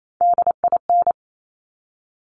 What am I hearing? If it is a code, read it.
Morse code, heard as a keyed tone, about 28 wpm, 700 Hz. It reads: DIN
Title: Top Three Letter Combinations in Words 28wpm